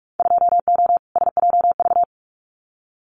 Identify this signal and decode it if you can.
Morse code, heard as a keyed tone, about 46 wpm, 700 Hz. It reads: 2O SJV
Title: Winter Field Day Exchanges Encore 45wpm